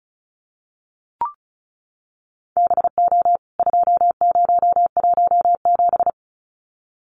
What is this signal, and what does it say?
Morse code, heard as a keyed tone, about 34 wpm, 700 Hz. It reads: BO 2017
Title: CWT Exchange 35wpm